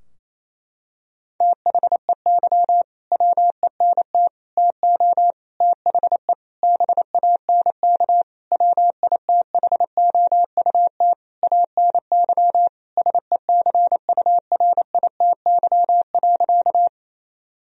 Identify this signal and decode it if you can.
Morse code, heard as a keyed tone, about 28 wpm, 700 Hz. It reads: THEY WENT TO THE BANK WITHOUT ANY SECURITY.
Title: Sentences from Top 1000 Words - Review Part 4 - 28wpm